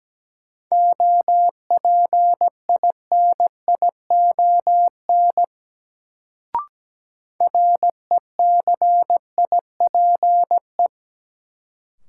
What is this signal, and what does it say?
Morse code, heard as a keyed tone, about 17 wpm, 700 Hz. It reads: OPINION RECIPE
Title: Sentences from Top 1200 Words - New Word Drill - 17wpm